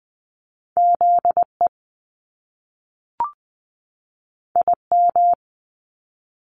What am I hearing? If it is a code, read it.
Morse code, heard as a keyed tone, about 20 wpm, 700 Hz. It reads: ZE IM
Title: Top Two Letter Combinations in Words 20wpm